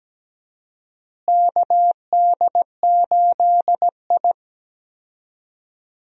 Morse code, heard as a keyed tone, about 17 wpm, 700 Hz. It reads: KD8I
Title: USA Call Signs - 4 Characters - Repeated 2x - 17wpm